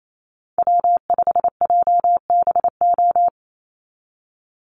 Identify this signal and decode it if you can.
Morse code, heard as a keyed tone, about 28 wpm, 700 Hz. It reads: W5JBO
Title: Call Signs Repeated 2x - 28wpm